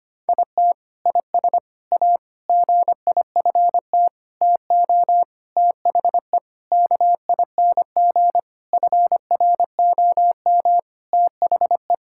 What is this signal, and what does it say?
Morse code, heard as a keyed tone, about 25 wpm, 700 Hz. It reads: IT IS A GIFT TO THE KING FROM THE
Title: Sentences from Top 1000 Words - Review Part 6 - 25wpm